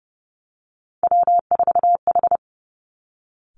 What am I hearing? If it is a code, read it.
Morse code, heard as a keyed tone, about 30 wpm, 700 Hz. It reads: W4H
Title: Three Challenging Letter-Number Combinations 30wpm